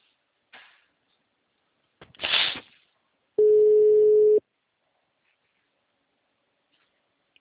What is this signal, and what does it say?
Ringback tone (Europe, 425 Hz).
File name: out- -2020- - - .64